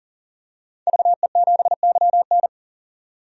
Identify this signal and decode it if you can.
Morse code, heard as a keyed tone, about 37 wpm, 700 Hz. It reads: VE7YN
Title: DXCC-100 Call Signs - 5 Characters - 40wpm